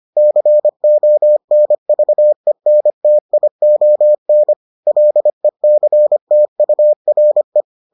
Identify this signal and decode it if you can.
Morse code, heard as a keyed tone, about 25 wpm, 600 Hz. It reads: CONVENTION LECTURE